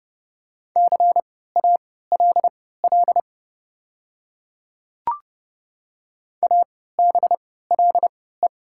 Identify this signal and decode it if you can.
Morse code, heard as a keyed tone, about 30 wpm, spaced as for 17 wpm, 700 Hz. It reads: CALL ABLE
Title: 3x Character Spacing - Top 500 Words - 30wpm